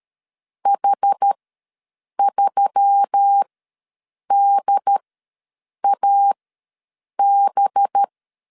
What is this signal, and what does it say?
Morse code, heard as a keyed tone, about 13 wpm, spaced as for 7 wpm, 800 Hz. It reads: H3DAB
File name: LEZIONI / 7 GF9Z / 7 GF ws 12wpm 3140